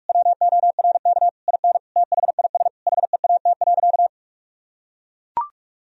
Morse code, heard as a keyed tone, about 43 wpm, 700 Hz. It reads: WORK IN THIS HEAT.
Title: Sentences from Top 1000 Words - Review Part 3 - 45wpm